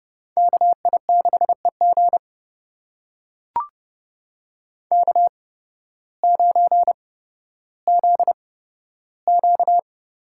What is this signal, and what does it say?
Morse code, heard as a keyed tone, about 30 wpm, spaced as for 14 wpm, 700 Hz. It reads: KI6EG K9ZQ
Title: USA Call Signs - 8x Character Spacing - 30wpm